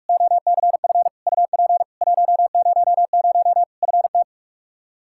Morse code, heard as a keyed tone, about 46 wpm, 700 Hz. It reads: OCF UP 100 FT
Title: Practice QSO - Exchange 2 - Info - 45wpm